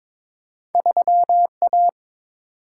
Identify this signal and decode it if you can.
Morse code, heard as a keyed tone, about 22 wpm, 700 Hz. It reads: 3A
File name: practice / 22wpm / Two Letter-Number Combinations - 22wpm